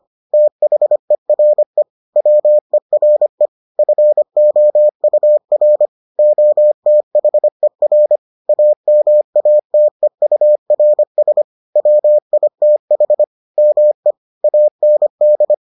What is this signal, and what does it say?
Morse code, heard as a keyed tone, about 25 wpm, 600 Hz. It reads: THERE WERE FOUR OTHER AMATEURS WITH ME AND